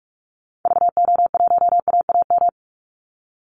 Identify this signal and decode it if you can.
Morse code, heard as a keyed tone, about 44 wpm, 700 Hz. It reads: VO1AAM
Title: Call Signs Repeated 2x - 45wpm